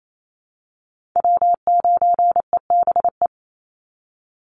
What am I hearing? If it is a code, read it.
Morse code, heard as a keyed tone, about 28 wpm, 700 Hz. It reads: W9EBE
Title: Call Signs Repeated 3x - 28wpm